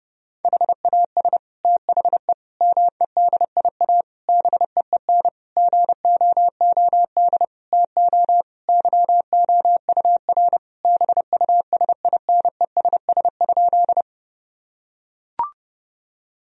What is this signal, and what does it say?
Morse code, heard as a keyed tone, about 30 wpm, 700 Hz. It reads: HAS THE MEDIA BEEN GOOD TO YOUR BUSINESS?
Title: Sentences from Top 1000 Words - Review Part 4 - 30wpm